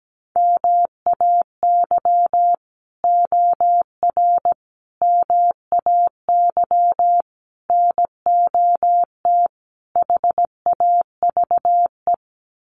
Morse code, heard as a keyed tone, about 17 wpm, 700 Hz. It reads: MAY OR MAY NOT HAVE
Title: Sets of 5 Words Encore2 17wpm